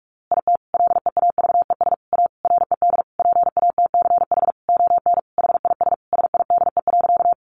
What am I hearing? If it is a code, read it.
Morse code, heard as a keyed tone, about 44 wpm, 700 Hz. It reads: IT LEAVES A RED PATCH ON HIS SIDE.
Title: A Sound Of Thunder 45wpm